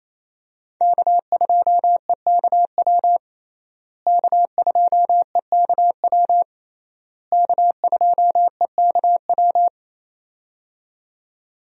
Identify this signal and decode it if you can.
Morse code, heard as a keyed tone, about 28 wpm, 700 Hz. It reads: K2EKW K2EKW K2EKW
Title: USA Call Signs - 5 Characters - Repeated 3x - 28wpm